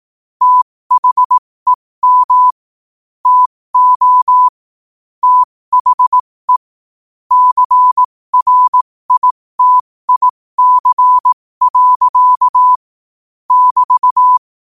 Morse code, heard as a keyed tone, about 18 wpm, spaced as for 15 wpm, 1000 Hz. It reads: THEM TO THE CRITIC. <BT>
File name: qotdfiles / 15wpm